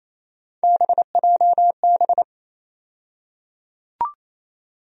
Morse code, heard as a keyed tone, about 28 wpm, 700 Hz. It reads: BJB
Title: USA Call Sign Postfixes - 3 Letters 28wpm